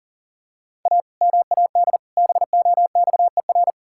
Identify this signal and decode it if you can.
Morse code, heard as a keyed tone, about 40 wpm, 700 Hz. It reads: A MAD BOXER
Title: Pangram Challenge v2 - 40wpm